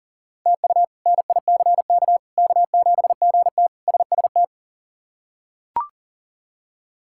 Morse code, heard as a keyed tone, about 37 wpm, 700 Hz. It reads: TU NICK K7GT SST
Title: K1USN SST QSO Practice - 40wpm